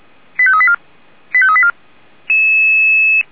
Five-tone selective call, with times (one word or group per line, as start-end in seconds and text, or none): 0.4-0.8 85384
1.3-1.7 85384